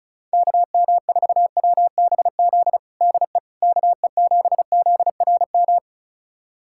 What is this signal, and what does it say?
Morse code, heard as a keyed tone, about 36 wpm, 700 Hz. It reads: KM4WBZ DE KE7ZRM
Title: USA Callsign DE Callsign 35wpm